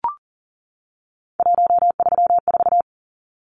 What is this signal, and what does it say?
Morse code, heard as a keyed tone, about 40 wpm, 700 Hz. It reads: 134
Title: Numbers - 3-Digit 40wpm